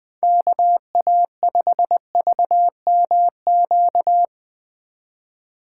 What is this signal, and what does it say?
Morse code, heard as a keyed tone, about 20 wpm, 700 Hz. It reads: KA5VMQ
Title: Speed Racing - USA 2x3 Call Signs 17wpm